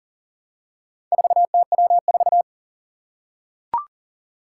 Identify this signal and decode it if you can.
Morse code, heard as a keyed tone, about 37 wpm, 700 Hz. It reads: 4TW4
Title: Four Challenging Letter-Number Combinations 40wpm